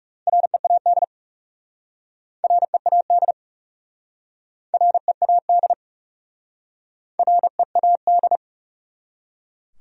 Morse code, heard as a keyed tone, about 40 wpm, 700 Hz. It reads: READ READ READ READ